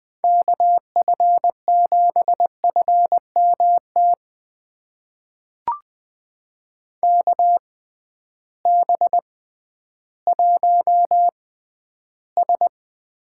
Morse code, heard as a keyed tone, about 20 wpm, spaced as for 12 wpm, 700 Hz. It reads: KF7FMT KB1S
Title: USA Call Signs - 6 Characters - 6x Character Spacing - 20wpm